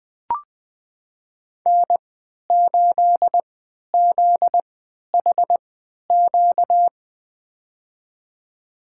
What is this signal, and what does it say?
Morse code, heard as a keyed tone, about 20 wpm, spaced as for 11 wpm, 700 Hz. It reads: N8ZHQ
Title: USA Call Signs - 5 Characters - 3x Character Spacing - 20wpm